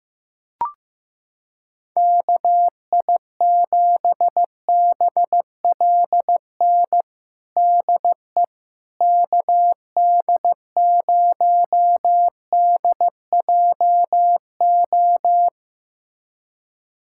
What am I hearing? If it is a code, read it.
Morse code, heard as a keyed tone, about 15 wpm, 700 Hz. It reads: KI7BLN DE KD0DJO
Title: USA Callsign DE Callsign 15wpm